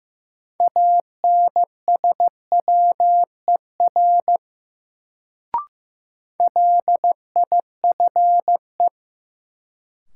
Morse code, heard as a keyed tone, about 15 wpm, 700 Hz. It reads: ANSWER LIFE
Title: Sentences from Top 300 Words - New Word Drill - 15wpm